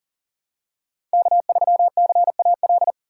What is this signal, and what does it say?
Morse code, heard as a keyed tone, about 40 wpm, 700 Hz. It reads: K3CAL
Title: USA Call Signs - 3x Character Spacing - 40wpm